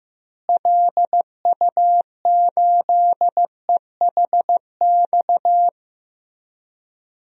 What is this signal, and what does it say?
Morse code, heard as a keyed tone, about 15 wpm, 700 Hz. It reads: LU8EHX
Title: DXCC-100 Call Signs - 6 Characters - Repeated 3x - 15wpm